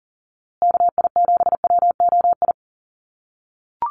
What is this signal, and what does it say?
Morse code, heard as a keyed tone, about 40 wpm, 700 Hz. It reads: KI7WOI
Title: USA Call Signs Repeated 2x 40wpm